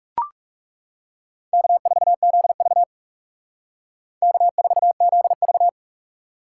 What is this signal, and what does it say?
Morse code, heard as a keyed tone, about 38 wpm, 700 Hz. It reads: K4ZV K4ZV